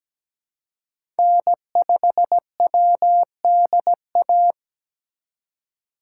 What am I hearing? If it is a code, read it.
Morse code, heard as a keyed tone, about 17 wpm, 700 Hz. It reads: N5WDA
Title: USA Call Signs - 5 Characters - 2x Character Spacing - 17wpm